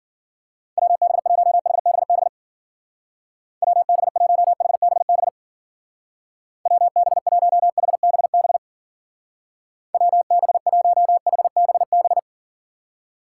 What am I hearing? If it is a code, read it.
Morse code, heard as a keyed tone, about 48 wpm, 700 Hz. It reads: WB1HBB WB1HBB WB1HBB WB1HBB
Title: Speed Racing - USA 2x3 Call Signs 40wpm